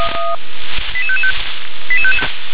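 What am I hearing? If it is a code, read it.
Five-tone selective call (ZVEI), heard as quick runs of five tones, tones of about 70 ms. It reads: 88555
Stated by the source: ZVEI five-tone selective call received on SDR.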